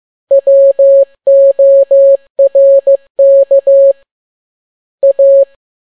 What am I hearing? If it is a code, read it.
Morse code, heard as a keyed tone, about 15 wpm, 550 Hz. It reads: WORK A